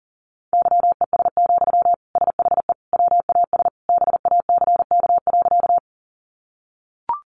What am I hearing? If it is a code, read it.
Morse code, heard as a keyed tone, about 40 wpm, 700 Hz. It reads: YES, SHE WAS BACK.
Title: Sentences from Top 500 Words 40wpm